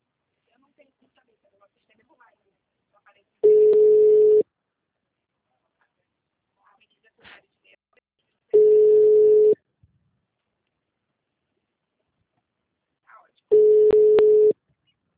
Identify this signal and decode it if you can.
Ringback tone (Europe, 425 Hz), with speech.
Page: Rio 3 - Ura